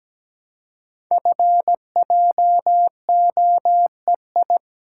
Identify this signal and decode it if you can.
Morse code, heard as a keyed tone, about 17 wpm, 700 Hz. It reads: FJOEI
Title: Code Groups - 5 Letters - 17wpm